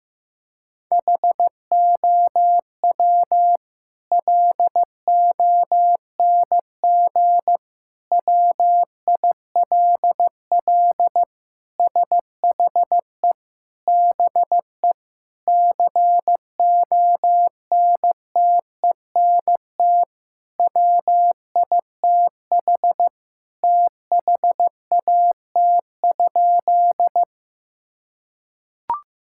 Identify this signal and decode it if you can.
Morse code, heard as a keyed tone, about 15 wpm, 700 Hz. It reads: HOW LONG WILL SHE BE CONTENT WITH THAT?
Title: Sentences from Top 1000 Words - Review Part 6 - 15wpm